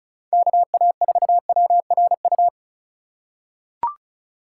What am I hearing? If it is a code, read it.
Morse code, heard as a keyed tone, about 36 wpm, 700 Hz. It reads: KA4WRU
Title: USA Call Signs - 6 Characters - 2x Character Spacing - 35wpm